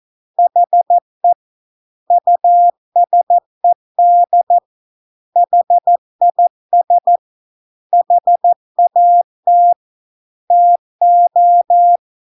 Morse code, heard as a keyed tone, about 14 wpm, 700 Hz. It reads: HE USED HIS HAT TO